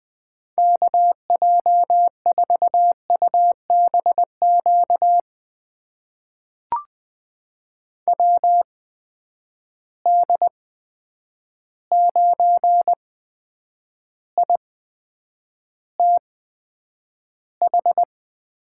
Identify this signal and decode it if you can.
Morse code, heard as a keyed tone, about 20 wpm, spaced as for 9 wpm, 700 Hz. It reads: KJ4UBQ WD9ITH
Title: USA Call Signs - 6 Characters - 8x Character Spacing - 20wpm